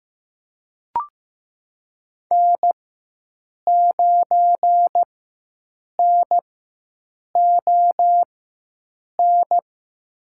Morse code, heard as a keyed tone, about 15 wpm, spaced as for 7 wpm, 700 Hz. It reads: N9NON
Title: USA Call Signs - 5 Characters - 4x Character Spacing - 15wpm